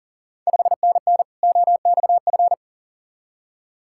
Morse code, heard as a keyed tone, about 37 wpm, 690 Hz. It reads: 5NN OXF
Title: Ontario QSO Party Exchange 40wpm